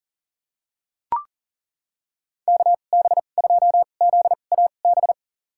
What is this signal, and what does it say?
Morse code, heard as a keyed tone, about 40 wpm, spaced as for 29 wpm, 700 Hz. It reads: KB2ZAB
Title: USA Call Signs - 6 Characters - 2x Character Spacing - 40wpm